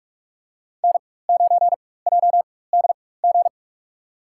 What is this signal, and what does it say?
Morse code, heard as a keyed tone, about 44 wpm, spaced as for 21 wpm, 700 Hz. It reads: N9JDG
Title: USA Call Signs - 5 Characters - 4x Character Spacing - 45wpm